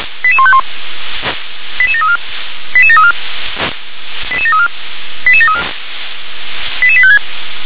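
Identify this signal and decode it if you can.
Five-tone selective call (ZVEI), heard as quick runs of five tones, tones of about 70 ms. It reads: 88151 88734 88734 88734 88734 88856
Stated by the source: ZVEI five-tone selective call received on an SDR.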